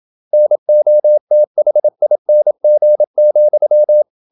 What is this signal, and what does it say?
Morse code, heard as a keyed tone, about 27 wpm, 600 Hz. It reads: NOTHING,
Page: Never 26wpm | CW med Gnister